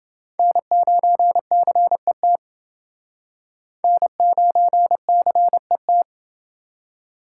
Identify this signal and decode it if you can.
Morse code, heard as a keyed tone, about 32 wpm, 700 Hz. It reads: N9CET N9CET